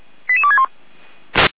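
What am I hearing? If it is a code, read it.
Five-tone selective call (ZVEI), heard as quick runs of five tones, tones of about 70 ms. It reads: 88261
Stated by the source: ZVEI five-tone call received on an SDR.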